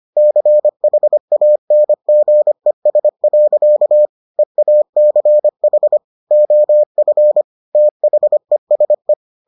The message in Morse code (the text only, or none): CHANGES. EACH OF THESE